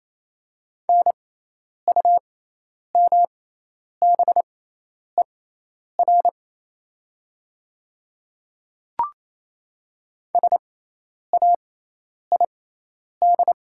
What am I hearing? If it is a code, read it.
Morse code, heard as a keyed tone, about 27 wpm, spaced as for 10 wpm, 700 Hz. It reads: NUMBER SAID
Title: 6x Character Spacing - Top 100 Words - 28wpm